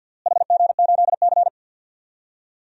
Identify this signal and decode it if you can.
Morse code, heard as a keyed tone, about 46 wpm, 700 Hz. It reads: HG8C
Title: DXCC-100 Call Signs - 4 Characters - Repeated 3x - 50wpm